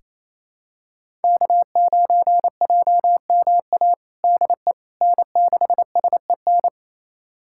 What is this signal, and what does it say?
Morse code, heard as a keyed tone, about 28 wpm, 700 Hz. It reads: K9JMA DE N6SEN
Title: USA Callsign DE Callsign - 5 Characters - 28wpm